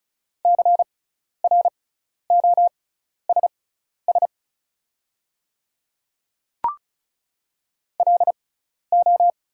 Morse code, heard as a keyed tone, about 36 wpm, spaced as for 12 wpm, 700 Hz. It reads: CROSS LO
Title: 6x Character Spacing - Top 500 Words - 35wpm